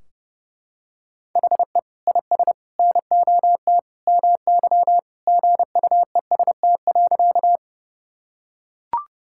Morse code, heard as a keyed tone, about 30 wpm, 700 Hz. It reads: HE IS NOT MY GUEST.
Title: Sentences from Top 1600 Words 30wpm